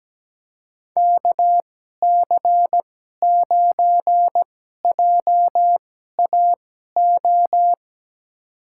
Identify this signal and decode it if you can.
Morse code, heard as a keyed tone, about 17 wpm, spaced as for 12 wpm, 700 Hz. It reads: KC9JAO
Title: USA Call Signs - 6 Characters - 2x Character Spacing - 17wpm